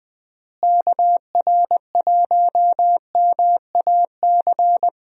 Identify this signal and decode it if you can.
Morse code, heard as a keyed tone, about 20 wpm, 700 Hz. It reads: KR1MAC
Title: USA Call Signs - 6 Characters - Repeated 3x - 20wpm